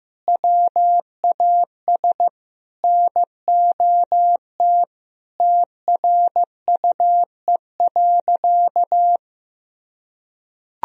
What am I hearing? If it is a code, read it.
Morse code, heard as a keyed tone, about 16 wpm, 700 Hz. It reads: WAS NOT TRUE.
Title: Sentences from Top 1000 Words - Review Part 4 - 15wpm